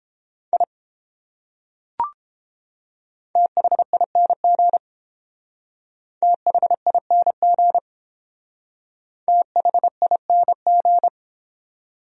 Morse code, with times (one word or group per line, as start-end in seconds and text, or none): 0.5-0.6 I
3.4-4.8 THING
6.2-7.8 THING
9.3-11.1 THING